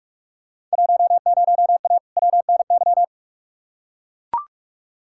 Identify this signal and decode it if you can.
Morse code, heard as a keyed tone, about 42 wpm, 700 Hz. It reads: 10A WNY
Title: Field Day Exchanges - 45wpm